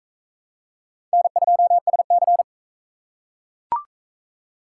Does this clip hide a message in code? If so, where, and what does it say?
Yes: Morse code from 1.1 to 2.4 s, heard as a keyed tone, about 41 wpm, 700 Hz. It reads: N2SC